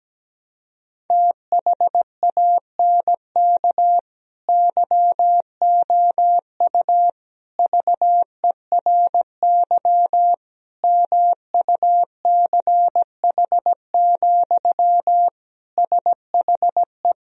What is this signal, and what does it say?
Morse code, heard as a keyed tone, about 17 wpm, 700 Hz. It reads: THANK YOU VERY MUCH, SHE
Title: Sentences from Top 1000 Words - Review Part 4 - 17wpm